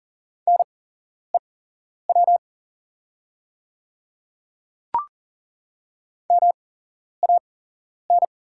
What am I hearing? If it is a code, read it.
Morse code, heard as a keyed tone, about 40 wpm, spaced as for 11 wpm, 690 Hz. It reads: NEW MAN
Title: 8x Character Spacing - Top 300 Words - 40wpm